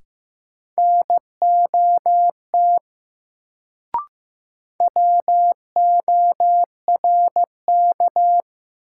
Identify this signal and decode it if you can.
Morse code, heard as a keyed tone, about 15 wpm, 700 Hz. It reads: NOT WORK
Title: Sets of 2 Words - New Word Drill - 15wpm